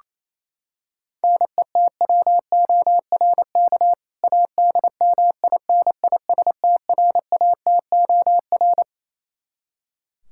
Morse code, heard as a keyed tone, about 28 wpm, 700 Hz. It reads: NETWORK ADMINISTRATOR
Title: K1USN SST QSO Practice 28wpm